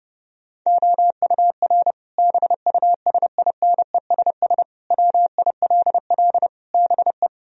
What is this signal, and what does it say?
Morse code, heard as a keyed tone, about 30 wpm, 700 Hz. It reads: OUR BUSINESS WILL BE
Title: Sentences from Top 1000 Words - Review Part 4 - 30wpm